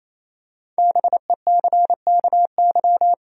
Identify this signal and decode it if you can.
Morse code, heard as a keyed tone, about 28 wpm, 700 Hz. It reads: BECKY
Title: USA Common Names - Max 5 Letters - 28wpm